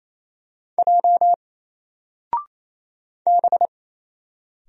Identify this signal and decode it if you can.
Morse code, heard as a keyed tone, about 27 wpm, 700 Hz. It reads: J B
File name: Single-Letter-Number-28wpm